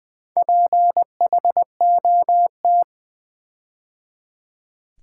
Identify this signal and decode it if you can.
Morse code, heard as a keyed tone, about 20 wpm, 700 Hz. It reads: PHOT
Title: Build A Word - 4 Letters At A Time - 10-Letter Words - 20wpm